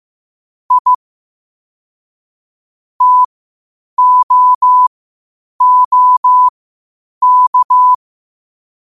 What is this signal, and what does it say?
Morse code, heard as a keyed tone, about 15 wpm, spaced as for 8 wpm, 1000 Hz. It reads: I TOOK
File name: qotdfiles / 8wpm